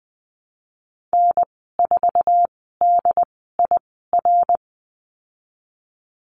Morse code, heard as a keyed tone, about 20 wpm, spaced as for 15 wpm, 700 Hz. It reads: N4DIR
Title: 088 Lesson 15g - Number 5 - Call Signs 20wpm